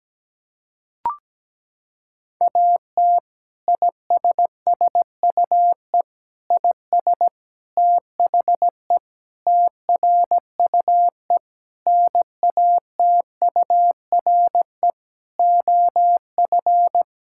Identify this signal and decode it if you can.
Morse code, heard as a keyed tone, about 17 wpm, 700 Hz. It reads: AT ISSUE IS THE TRUE NATURE OF
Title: Sentences from Top 1000 Words - Review Part 6 - 17wpm